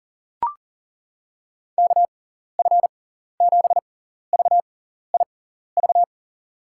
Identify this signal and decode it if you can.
Morse code, heard as a keyed tone, about 40 wpm, spaced as for 14 wpm, 700 Hz. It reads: KF7VIV
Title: USA Call Signs - 6 Characters - 6x Character Spacing - 40wpm